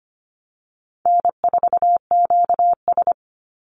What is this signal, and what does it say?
Morse code, heard as a keyed tone, about 25 wpm, 700 Hz. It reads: N4QS
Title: Easy Call Signs Repeated 2x - 25wpm